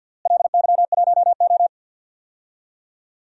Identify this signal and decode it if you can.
Morse code, heard as a keyed tone, about 50 wpm, 700 Hz. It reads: LY1O
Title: DXCC-100 Call Signs - 4 Characters - Repeated 3x - 50wpm